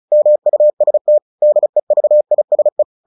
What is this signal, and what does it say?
Morse code, heard as a keyed tone, about 34 wpm, 600 Hz. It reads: MUST DEVISE
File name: the old man and the sea 2 35wpm